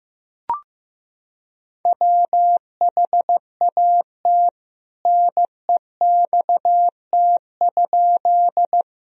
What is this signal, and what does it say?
Morse code, heard as a keyed tone, about 15 wpm, 700 Hz. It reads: WHAT NEXT?
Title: Sentences from Top 1000 Words - Review Part 1 - 15wpm